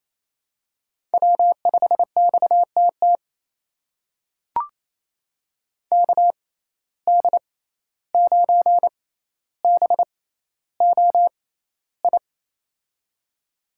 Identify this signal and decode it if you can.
Morse code, heard as a keyed tone, about 28 wpm, spaced as for 14 wpm, 700 Hz. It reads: W5XTT KD9BOI
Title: USA Call Signs - 6x Character Spacing - 28wpm